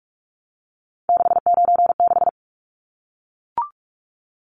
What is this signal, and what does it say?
Morse code, heard as a keyed tone, about 44 wpm, 700 Hz. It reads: 696
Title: Numbers - 3-Digit 45wpm